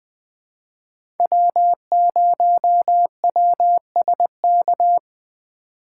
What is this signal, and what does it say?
Morse code, heard as a keyed tone, about 20 wpm, 700 Hz. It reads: W0WSK
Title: USA Call Signs - 5 Characters - 3x Character Spacing - 20wpm